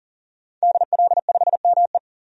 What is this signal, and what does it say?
Morse code, heard as a keyed tone, about 40 wpm, 700 Hz. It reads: DL5ME
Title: DXCC-100 Call Signs - 5 Characters - 40wpm